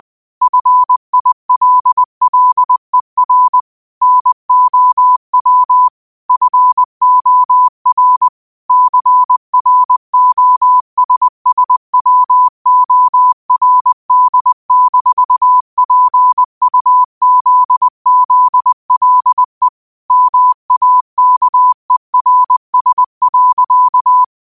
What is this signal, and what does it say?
Morse code, heard as a keyed tone, about 20 wpm, 1000 Hz. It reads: FILLER NOW FOR CROSSWORD-PUZZLE MAKERS.